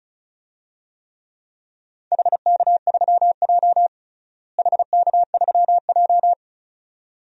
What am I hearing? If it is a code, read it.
Morse code, heard as a keyed tone, about 34 wpm, 690 Hz. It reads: HK3J HK3J
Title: DXCC-100 Call Signs - 4 Characters - Repeated 2x - 35wpm